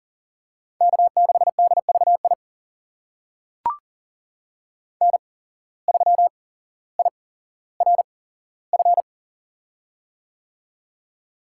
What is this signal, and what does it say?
Morse code, heard as a keyed tone, about 38 wpm, spaced as for 17 wpm, 700 Hz. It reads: K6DVI N3IRF
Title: USA Call Signs - 5 Characters - 8x Character Spacing - 40wpm